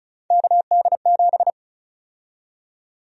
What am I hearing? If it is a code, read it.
Morse code, heard as a keyed tone, about 36 wpm, 700 Hz. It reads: KD7
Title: USA Call Sign Prefixes - Repeated 2x - 35wpm